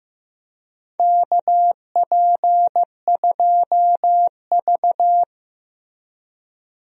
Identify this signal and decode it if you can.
Morse code, heard as a keyed tone, about 15 wpm, 700 Hz. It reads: KP2V
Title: DXCC-100 Call Signs - 4 Characters - Repeated 3x - 15wpm